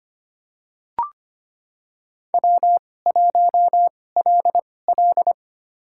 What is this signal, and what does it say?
Morse code, heard as a keyed tone, about 25 wpm, spaced as for 18 wpm, 700 Hz. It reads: W1LL
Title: USA Call Signs - 4 Characters - 2x Character Spacing - 25wpm